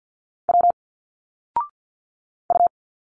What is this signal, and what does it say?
Morse code, heard as a keyed tone, about 51 wpm, 700 Hz. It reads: W U
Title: Single Letters 50wpm